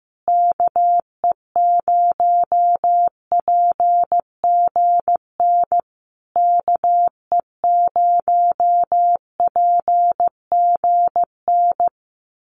Morse code, heard as a keyed tone, about 15 wpm, 700 Hz. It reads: KE0PGN KE0PGN
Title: USA Call Signs Repeated 3x 15wpm